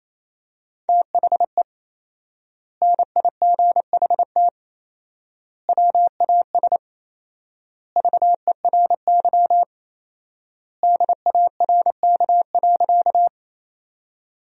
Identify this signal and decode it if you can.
Morse code, heard as a keyed tone, about 27 wpm, 700 Hz. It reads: THE NIGHT WAS VERY DARK.
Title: 4x Word-Spacing - Sentences from Top 500 Words - 28wpm